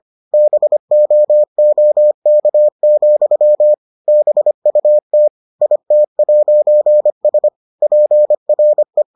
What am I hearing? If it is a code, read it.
Morse code, heard as a keyed tone, about 25 wpm, 600 Hz. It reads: BOOK, BUT IT'S PRE